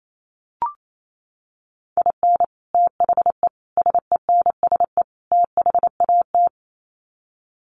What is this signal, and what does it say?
Morse code, heard as a keyed tone, about 28 wpm, 700 Hz. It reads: IN THE SENSE THAT
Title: Sets of 4 Words 28wpm